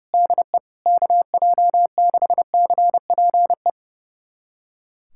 Morse code, heard as a keyed tone, about 30 wpm, 700 Hz. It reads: DE KJ6CPE
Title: USA Callsign DE Callsign 30wpm